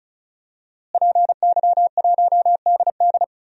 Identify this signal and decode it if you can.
Morse code, heard as a keyed tone, about 36 wpm, 700 Hz. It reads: PY1DD
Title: DXCC-100 Call Signs - 5 Characters - Repeated 3x - 35wpm